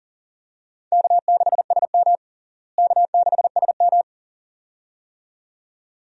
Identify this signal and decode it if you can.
Morse code, heard as a keyed tone, about 40 wpm, 700 Hz. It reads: K6SM K6SM
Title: USA Call Signs - 4 Characters - Repeated 2x - 40wpm